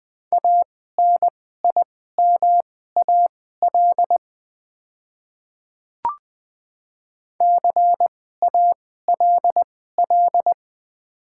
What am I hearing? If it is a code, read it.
Morse code, heard as a keyed tone, about 20 wpm, spaced as for 15 wpm, 700 Hz. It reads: ANIMAL CALL
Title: 2x Character Spacing - Top 300 Words - 20wpm